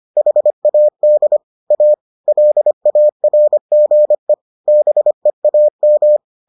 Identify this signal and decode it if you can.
Morse code, heard as a keyed tone, about 25 wpm, 600 Hz. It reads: HAD A LARGE BEAM